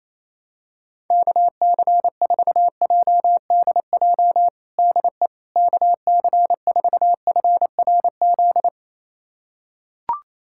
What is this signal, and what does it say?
Morse code, heard as a keyed tone, about 28 wpm, 700 Hz. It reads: KC4JDJ DE KC4FRZ
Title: USA Callsign DE Callsign - 6 Characters - 28wpm